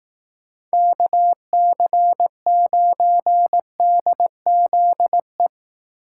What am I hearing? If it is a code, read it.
Morse code, heard as a keyed tone, about 18 wpm, 700 Hz. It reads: KC9DZE
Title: Speed Racing - USA Call Signs 15wpm